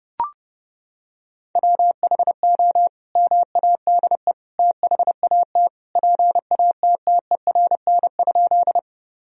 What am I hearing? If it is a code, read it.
Morse code, heard as a keyed tone, about 30 wpm, 700 Hz. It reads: WHO MADE THAT PATTERN?
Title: Sentences from Top 1000 Words - Review Part 2 - 30wpm